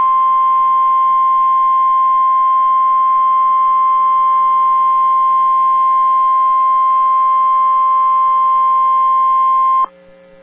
Emergency Alert System (SAME) tones (the 1050 Hz attention tone). It is heard from 0.8 to 9.8 s.